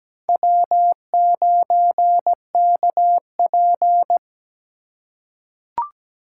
Morse code, heard as a keyed tone, about 17 wpm, 700 Hz. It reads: W9KP
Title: USA Call Signs - 4x Character Spacing - 17wpm